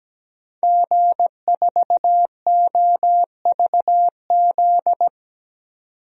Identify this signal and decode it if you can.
Morse code, heard as a keyed tone, about 17 wpm, 700 Hz. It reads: G4OVZ
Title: DXCC-100 Call Signs - 5 Characters - 17wpm